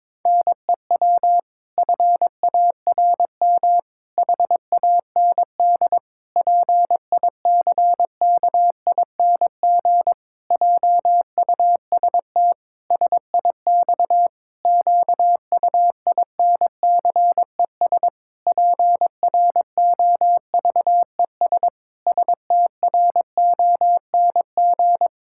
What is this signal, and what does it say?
Morse code, heard as a keyed tone, about 22 wpm, 700 Hz. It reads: NEW FARM HAND PICKING JUST SIX QUINCES PROVES STRONG
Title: Pangram Challenge v2 - 22wpm